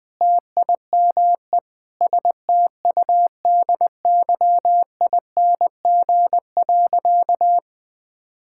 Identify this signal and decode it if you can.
Morse code, heard as a keyed tone, about 21 wpm, 700 Hz. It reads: TIME STUDYING.
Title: Sentences with 6-Word Phrases - 20wpm